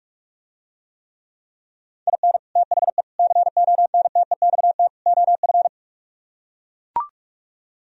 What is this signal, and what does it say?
Morse code, heard as a keyed tone, about 49 wpm, 700 Hz. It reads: IN THE CONTEXT OF